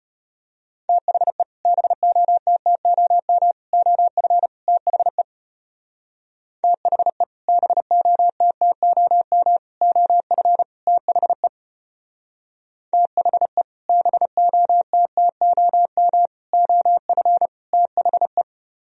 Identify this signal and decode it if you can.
Morse code, heard as a keyed tone, about 32 wpm, 690 Hz. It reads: THE BOTTOM OF THE THE BOTTOM OF THE THE BOTTOM OF THE